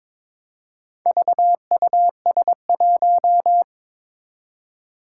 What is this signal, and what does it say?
Morse code, heard as a keyed tone, about 22 wpm, 700 Hz. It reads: VUS1
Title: Four Challenging Letter-Number Combinations 22wpm